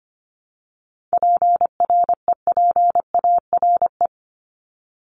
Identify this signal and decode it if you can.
Morse code, heard as a keyed tone, about 25 wpm, 700 Hz. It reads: PREPARE
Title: ED Suffix - Top 100 Words with Base Words 25wpm